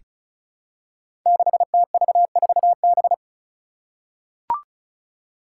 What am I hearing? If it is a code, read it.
Morse code, heard as a keyed tone, about 34 wpm, 700 Hz. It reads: 6TV4B
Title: Five Challenging Letter-Number Combinations 35wpm